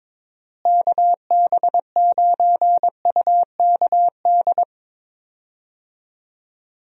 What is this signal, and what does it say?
Morse code, heard as a keyed tone, about 22 wpm, 700 Hz. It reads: KB9UKD
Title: USA Call Signs - 6 Characters - Repeated 3x - 22wpm